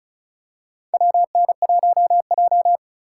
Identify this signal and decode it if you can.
Morse code, heard as a keyed tone, about 34 wpm, 700 Hz. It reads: WN1J
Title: USA Call Signs - 4 Characters - 2x Character Spacing - 35wpm